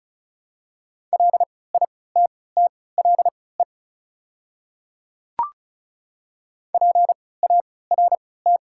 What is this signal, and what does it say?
Morse code, heard as a keyed tone, about 36 wpm, spaced as for 20 wpm, 690 Hz. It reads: LITTLE PART
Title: 3x Character Spacing - Top 200 Words - 35wpm